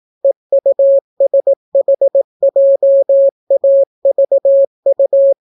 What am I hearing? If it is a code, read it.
Morse code, heard as a keyed tone, about 18 wpm, 550 Hz. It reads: EUSHJAVU